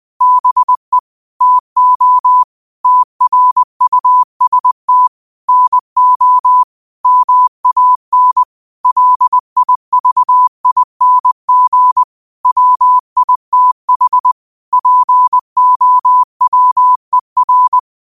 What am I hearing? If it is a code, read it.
Morse code, heard as a keyed tone, about 20 wpm, 1000 Hz. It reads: BE TO TRUST NO MAN LIVING WITH POWER